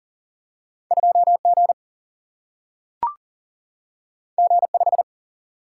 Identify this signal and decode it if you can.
Morse code, heard as a keyed tone, about 39 wpm, 700 Hz. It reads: 2 G G 5
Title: Single-Letter Single-Number 40wpm